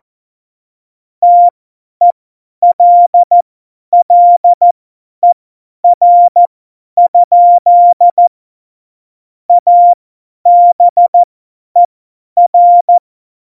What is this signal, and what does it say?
Morse code, heard as a keyed tone, about 14 wpm, spaced as for 10 wpm, 700 Hz. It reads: TELLER? ABER
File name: Morsekurs / Witze / Witz 02. Juli 2019, 50 F70